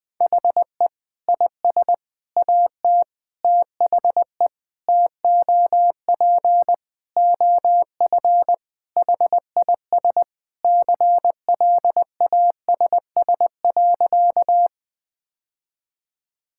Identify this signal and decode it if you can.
Morse code, heard as a keyed tone, about 20 wpm, 700 Hz. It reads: HE IS AT THE TOP OF HIS CLASS.
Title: Sentences with 4-Word Phrases - Spoken Beginning - 20wpm